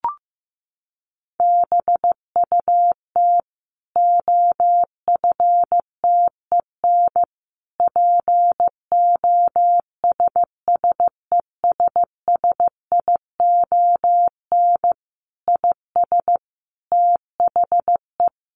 Morse code, heard as a keyed tone, about 15 wpm, 700 Hz. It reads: BUT OFTEN POSSESSION IS THE
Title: Sentences from Top 1400 Words 15wpm